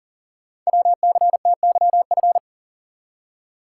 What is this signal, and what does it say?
Morse code, heard as a keyed tone, about 39 wpm, 700 Hz. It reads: WCTYF
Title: Code Groups - 5 Letter:Numbers - 40wpm